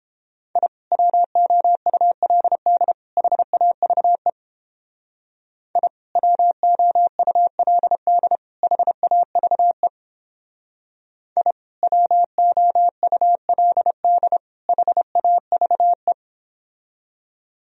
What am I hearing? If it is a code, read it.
Morse code, heard as a keyed tone, about 29 wpm, 700 Hz. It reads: I WOULD HAVE I WOULD HAVE I WOULD HAVE